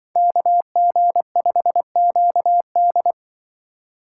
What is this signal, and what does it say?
Morse code, heard as a keyed tone, about 25 wpm, 700 Hz. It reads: KG5QD
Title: Speed Racing - USA 2x2 Call Signs 20wpm